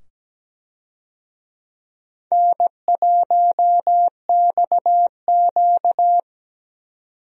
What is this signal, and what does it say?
Morse code, heard as a keyed tone, about 17 wpm, 700 Hz. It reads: N1XQ
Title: USA Call Signs - 4 Characters - 17wpm